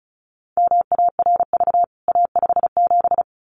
Morse code, heard as a keyed tone, about 34 wpm, 700 Hz. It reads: MARV A57
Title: CWT Exchange 35wpm